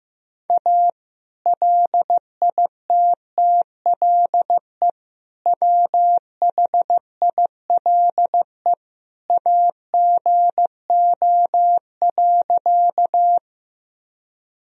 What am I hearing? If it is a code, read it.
Morse code, heard as a keyed tone, about 16 wpm, 700 Hz. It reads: A LITTLE WHILE AGO.
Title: Sentences from Top 1000 Words - Review Part 3 - 15wpm